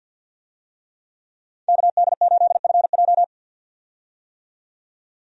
Morse code, heard as a keyed tone, about 52 wpm, 700 Hz. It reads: KD8FJ
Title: USA Call Signs - 5 Characters - 50wpm